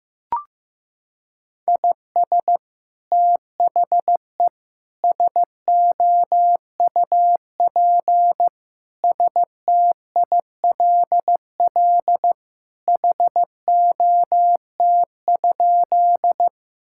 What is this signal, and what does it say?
Morse code, heard as a keyed tone, about 15 wpm, 700 Hz. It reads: IS THE SOUP STILL HOT?
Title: Sentences from Top 1000 Words - Review Part 6 - 15wpm